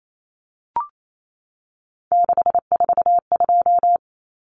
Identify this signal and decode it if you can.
Morse code, heard as a keyed tone, about 28 wpm, 700 Hz. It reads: 642
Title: Numbers - 3-Digit 28wpm